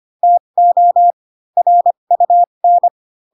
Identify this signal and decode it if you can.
Morse code, heard as a keyed tone, about 25 wpm, spaced as for 22 wpm, 700 Hz. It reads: TO RUN